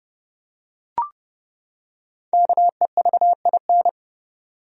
Morse code, heard as a keyed tone, about 30 wpm, 700 Hz. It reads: KEVIN
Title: USA Common Names - Max 5 Letters - 30wpm